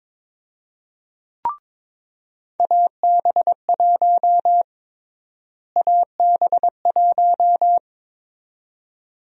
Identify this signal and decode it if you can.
Morse code, heard as a keyed tone, about 22 wpm, 700 Hz. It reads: AB1 AB1
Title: USA Call Sign Prefixes - Repeated 2x - 22wpm